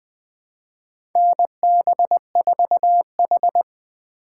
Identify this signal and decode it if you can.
Morse code, heard as a keyed tone, about 20 wpm, 700 Hz. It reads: NB4H
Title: USA Call Signs - 4 Characters - 3x Character Spacing - 20wpm